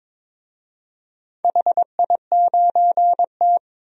Morse code, heard as a keyed tone, about 22 wpm, 700 Hz. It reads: HI9T
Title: DXCC-100 Call Signs - 4 Characters - 22wpm